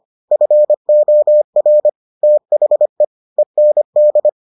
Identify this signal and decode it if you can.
Morse code, heard as a keyed tone, about 25 wpm, 600 Hz. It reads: FOR THE END